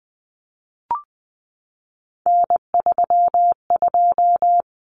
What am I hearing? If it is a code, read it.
Morse code, heard as a keyed tone, about 20 wpm, 700 Hz. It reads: N32
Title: Common Cut Numbers - 3-Digit 20wpm